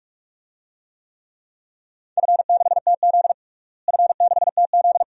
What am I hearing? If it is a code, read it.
Morse code, heard as a keyed tone, about 46 wpm, 700 Hz. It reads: F6TZ F6TZ
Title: DXCC-100 Call Signs - 4 Characters - Repeated 2x - 45wpm